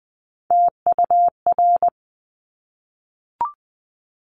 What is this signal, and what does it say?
Morse code, heard as a keyed tone, about 20 wpm, 700 Hz. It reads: TUR
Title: Top Three Letter Combinations in Words 20wpm